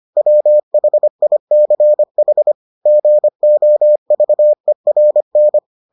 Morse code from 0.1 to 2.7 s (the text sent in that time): WHICH